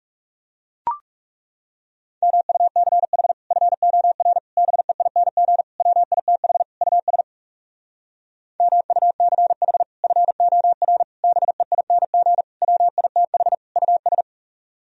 Morse code, heard as a keyed tone, about 40 wpm, 700 Hz. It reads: MUCH FOR BEING WITH US MUCH FOR BEING WITH US